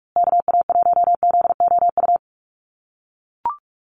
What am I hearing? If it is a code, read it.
Morse code, heard as a keyed tone, about 44 wpm, 700 Hz. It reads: KA1ZOU
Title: USA Call Signs Repeated 2x 45wpm